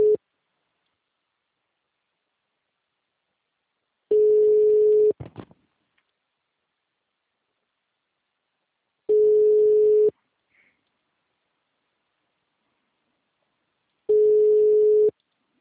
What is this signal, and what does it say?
Ringback tone (Europe, 425 Hz).